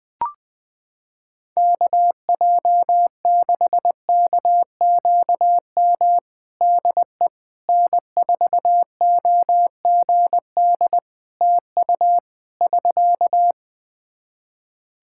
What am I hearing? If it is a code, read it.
Morse code, heard as a keyed tone, about 20 wpm, 700 Hz. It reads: KJ6KQM DE N4OGD TU <SK>
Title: Practice QSO - Exchange 4 - Ending - 20wpm